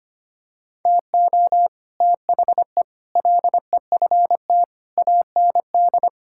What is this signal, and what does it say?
Morse code, heard as a keyed tone, about 25 wpm, 700 Hz. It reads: TO THE LEFT AND
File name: Sets of 9 Words - 25wpm